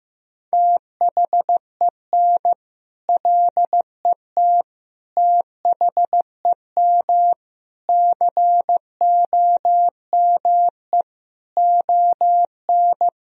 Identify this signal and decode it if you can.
Morse code, heard as a keyed tone, about 16 wpm, 700 Hz. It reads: THEN LET THEM COME ON
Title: Sentences from Top 1000 Words - Review Part 3 - 15wpm